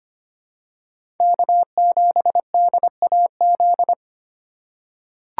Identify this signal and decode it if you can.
Morse code, heard as a keyed tone, about 25 wpm, 700 Hz. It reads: K7DAZ
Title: USA Call Signs - 5 Characters - 3x Character Spacing - 25wpm